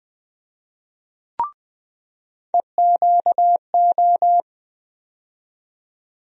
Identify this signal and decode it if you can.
Morse code, heard as a keyed tone, about 20 wpm, 700 Hz. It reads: EQO
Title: USA Call Sign Postfixes - 3 Letters 20wpm